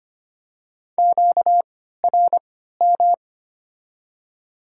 Morse code, heard as a keyed tone, about 25 wpm, spaced as for 14 wpm, 700 Hz. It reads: QRM
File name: Top 90 QSO Elements - 3x Character Spacing - 25wpm